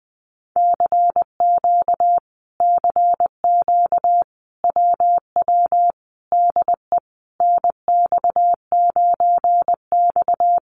Morse code, heard as a keyed tone, about 20 wpm, 700 Hz. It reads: CQ CQ WW DE NX9X
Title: Common QSO Phrases Encore 20wpm